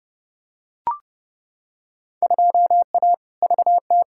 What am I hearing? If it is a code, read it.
Morse code, heard as a keyed tone, about 30 wpm, 700 Hz. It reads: 2A VT
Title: Field Day Exchanges Encore - 30wpm